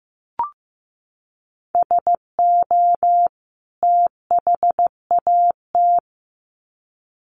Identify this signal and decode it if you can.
Morse code, heard as a keyed tone, about 14 wpm, 700 Hz. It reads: SO THAT
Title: Sets of 2 Words 15wpm